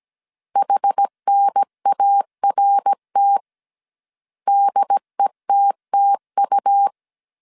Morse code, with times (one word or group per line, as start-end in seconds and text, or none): 0.6-3.4 HNART
4.5-6.9 DETTU